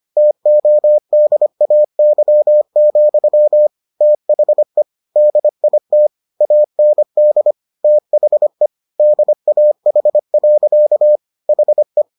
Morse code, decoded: TODAY, THE DIT AND THE DAH. HE

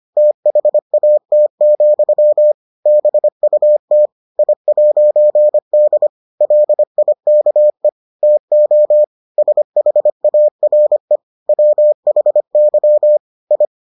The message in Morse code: THAT, BUT I'D LIKE TO SHARE WHY I

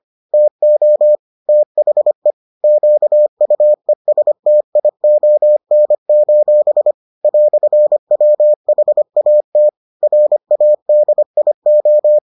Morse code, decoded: TO THE QUESTION: "WHAT RADIO

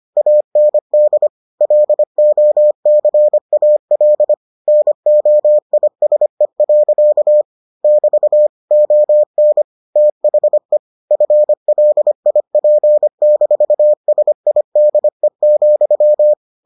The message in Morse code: AND LOCAL NOISE. <BT> ON THE FLIP-SIDE,

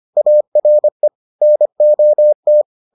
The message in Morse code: ARE NOT

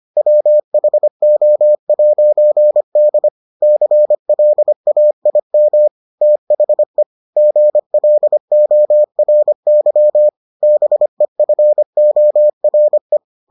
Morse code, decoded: WHO'D CLAIM THE GLORY BEFORE

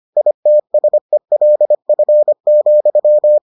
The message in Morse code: ITSELF,

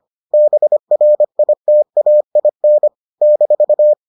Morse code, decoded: BRITAIN -

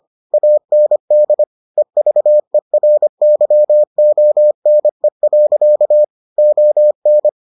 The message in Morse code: AND EVERYONE. ON

